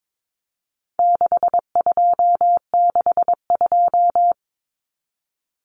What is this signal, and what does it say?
Morse code, heard as a keyed tone, about 22 wpm, 700 Hz. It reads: 6262
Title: Numbers - 4-Digit 22wpm